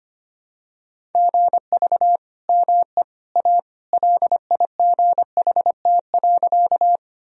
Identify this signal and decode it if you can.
Morse code, heard as a keyed tone, about 25 wpm, 700 Hz. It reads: GV ME A LIGHT.
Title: Sentences from Top 200 Words and Abbreviations 25wpm